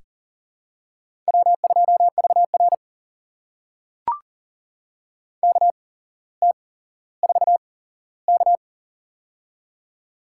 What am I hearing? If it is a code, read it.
Morse code, heard as a keyed tone, about 38 wpm, spaced as for 17 wpm, 700 Hz. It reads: W2VR KT4K
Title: USA Call Signs - 4 Characters - 8x Character Spacing - 40wpm